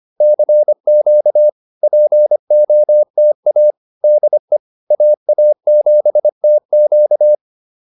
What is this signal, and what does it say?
Morse code, heard as a keyed tone, about 25 wpm, 600 Hz. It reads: CQ POTA DE AA7TQ